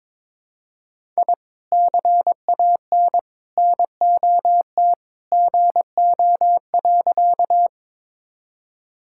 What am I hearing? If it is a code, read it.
Morse code, heard as a keyed tone, about 22 wpm, 700 Hz. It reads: I CAN NOT GO.
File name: practice / 22wpm / Sentences with 2-Word Phrases - 22wpm